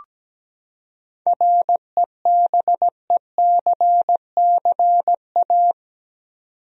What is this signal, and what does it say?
Morse code, heard as a keyed tone, about 17 wpm, 700 Hz. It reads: REBECCA
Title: USA Common Names - 17wpm